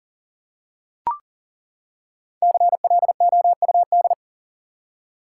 Morse code, heard as a keyed tone, about 40 wpm, 700 Hz. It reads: CLOUD
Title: Five Letter Words - 40wpm